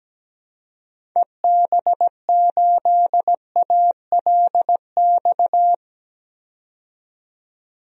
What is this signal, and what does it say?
Morse code, heard as a keyed tone, about 17 wpm, 700 Hz. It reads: EB8ALX
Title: DXCC-100 Call Signs - 6 Characters - 17wpm